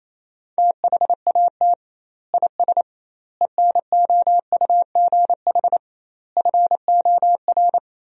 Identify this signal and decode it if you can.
Morse code, heard as a keyed tone, about 28 wpm, 700 Hz. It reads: THAT IS ENOUGH FOR
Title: 2x Word-Spacing - Sentences from Top 400 Words - 28wpm